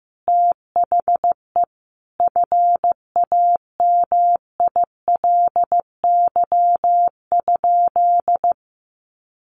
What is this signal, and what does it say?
Morse code, heard as a keyed tone, about 15 wpm, 700 Hz. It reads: THE FAMILY?
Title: Sentences from Top 1400 Words 15wpm